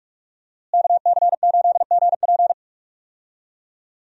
Morse code, heard as a keyed tone, about 44 wpm, 700 Hz. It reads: KC8GP
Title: USA Call Signs - 5 Characters - Repeated 3x - 45wpm